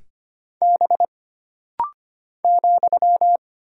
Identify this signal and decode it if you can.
Morse code, heard as a keyed tone, about 25 wpm, 700 Hz. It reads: B ,
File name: Single Character (Letter:Number:Punctuation:Prosign) - Rapid-Fire - 25wpm